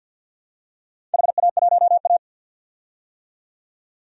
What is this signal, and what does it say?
Morse code, heard as a keyed tone, about 53 wpm, 700 Hz. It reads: HA1A
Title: DXCC-100 Call Signs - 4 Characters - 50wpm